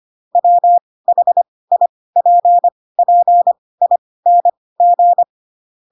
Morse code, heard as a keyed tone, about 25 wpm, spaced as for 18 wpm, 700 Hz. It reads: WHIPPING
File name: Pinocchio / Pinocchio ch7 12 25 ik1hge